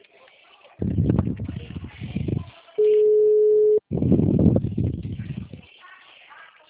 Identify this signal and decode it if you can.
Ringback tone (Europe, 425 Hz).
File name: out- -2016- - - .35075